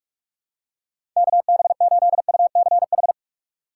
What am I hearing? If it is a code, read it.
Morse code, heard as a keyed tone, about 42 wpm, 700 Hz. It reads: KB8UCH
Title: USA Call Signs - 6 Characters - 8x Character Spacing - 45wpm